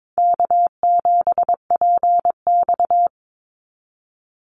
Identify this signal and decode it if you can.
Morse code, heard as a keyed tone, about 22 wpm, 700 Hz. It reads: K7PX
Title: NA SOTA Activator Callsigns 22wpm